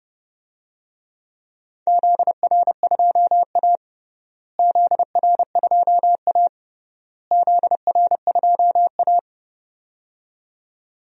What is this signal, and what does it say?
Morse code, heard as a keyed tone, about 30 wpm, 700 Hz. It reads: ZR2A ZR2A ZR2A
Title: DXCC-100 Call Signs - 4 Characters - Repeated 3x - 30wpm